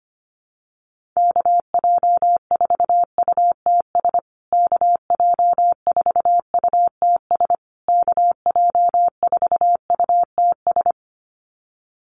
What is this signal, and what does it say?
Morse code, heard as a keyed tone, about 25 wpm, 700 Hz. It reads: KJ4UTS KJ4UTS KJ4UTS
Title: USA Call Signs Repeated 3x 25wpm